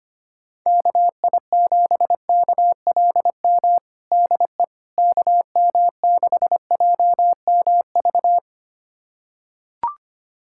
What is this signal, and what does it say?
Morse code, heard as a keyed tone, about 25 wpm, 700 Hz. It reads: KI7KLM DE KM6JMV
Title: USA Callsign DE Callsign - 6 Characters - 25wpm